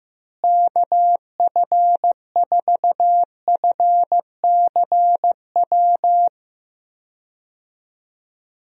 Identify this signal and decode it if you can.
Morse code, heard as a keyed tone, about 15 wpm, 700 Hz. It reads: KF4FCW
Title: USA Call Signs - 6 Characters - Repeated 3x - 15wpm